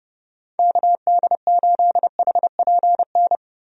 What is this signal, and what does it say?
Morse code, heard as a keyed tone, about 32 wpm, 700 Hz. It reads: KD8HPN
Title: USA Call Signs - 6 Characters - Repeated 3x - 30wpm